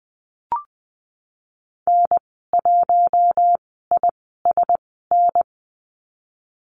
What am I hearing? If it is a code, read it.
Morse code, heard as a keyed tone, about 19 wpm, spaced as for 14 wpm, 700 Hz. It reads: N1ISN
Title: 055 Lesson 10g - Letter r - Call Signs 20wpm